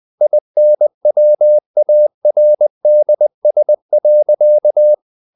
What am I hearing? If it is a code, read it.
Morse code, heard as a keyed tone, about 20 wpm, 600 Hz. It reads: INWARDS.
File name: the old man and the sea part 2 20wpm